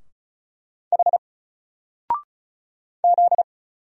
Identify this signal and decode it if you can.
Morse code, heard as a keyed tone, about 36 wpm, 700 Hz. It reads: H Z
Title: Single Letter-Number Alt 35wpm